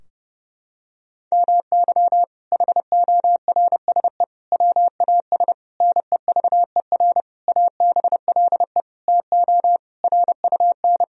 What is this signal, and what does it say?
Morse code, heard as a keyed tone, about 30 wpm, 700 Hz. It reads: MY HORSE WAS NEVER ABLE TO RUN
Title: Sentences from Top 1000 Words - Review Part 2 - 30wpm